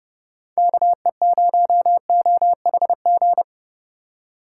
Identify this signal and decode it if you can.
Morse code, heard as a keyed tone, about 36 wpm, 700 Hz. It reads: KE0OHG
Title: Speed Racing - USA 2x3 Call Signs 30wpm